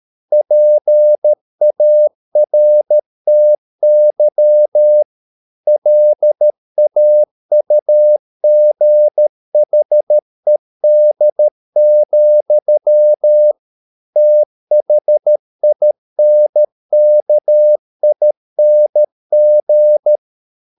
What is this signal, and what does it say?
Morse code, heard as a keyed tone, about 13 wpm, 600 Hz. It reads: PARTY LAUGHED, THINKING